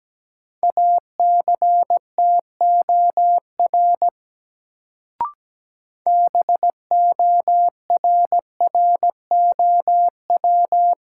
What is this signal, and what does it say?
Morse code, heard as a keyed tone, about 17 wpm, 700 Hz. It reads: ACTOR BORROW
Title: Sentences from Top 1500 Words - New Word Drill - 17wpm